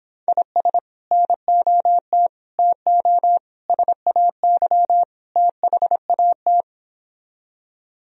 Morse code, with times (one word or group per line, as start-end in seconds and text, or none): 0.3-0.8 IS
1.1-2.3 NOT
2.6-3.4 TO
3.7-5.0 SAY
5.4-6.6 THAT